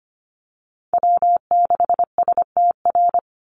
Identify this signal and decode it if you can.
Morse code, heard as a keyed tone, about 25 wpm, 700 Hz. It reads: W6STR
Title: NA SOTA Chaser Callsigns 25wpm